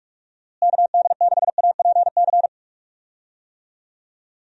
Morse code, heard as a keyed tone, about 43 wpm, 700 Hz. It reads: KD6APC
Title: USA Call Signs - 6 Characters - Repeated 2x - 45wpm